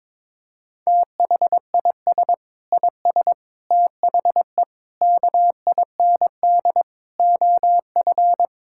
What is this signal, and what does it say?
Morse code, heard as a keyed tone, about 21 wpm, 700 Hz. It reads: THIS IS THE KIND OF
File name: practice / 22wpm / Sets of 7 Words - 22wpm